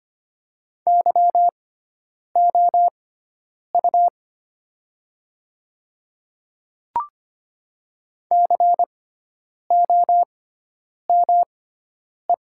Morse code, heard as a keyed tone, about 25 wpm, spaced as for 9 wpm, 700 Hz. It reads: YOU COME
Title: 6x Character Spacing - Top 100 Words - 25wpm